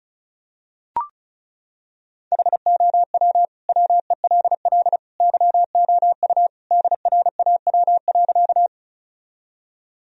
Morse code, heard as a keyed tone, about 34 wpm, 700 Hz. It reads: HOW WELL YOU DRAW.
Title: Sentences from Top 1000 Words - Review Part 1 - 35wpm